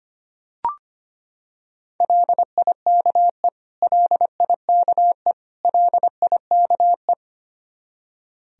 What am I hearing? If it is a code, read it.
Morse code, heard as a keyed tone, about 25 wpm, 700 Hz. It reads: LIKE LIKE LIKE
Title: Top 100 Words Repeated 25wpm